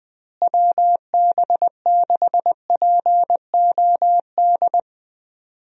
Morse code, heard as a keyed tone, about 20 wpm, 700 Hz. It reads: WB6POD
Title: USA Call Signs - 6 Characters - Repeated 3x - 20wpm